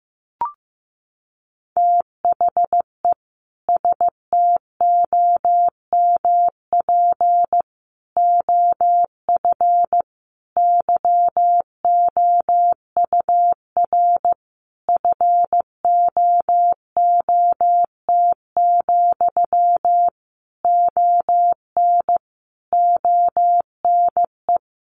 Morse code, heard as a keyed tone, about 15 wpm, 700 Hz. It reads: THE STOMP OF YOUR FOOT, ON ONE
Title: A Sound Of Thunder 15wpm